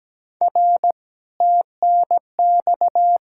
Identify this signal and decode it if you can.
Morse code, heard as a keyed tone, about 20 wpm, 700 Hz. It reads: R TNX
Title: Speed-Racing Common QSO Phrases Encore - 17wpm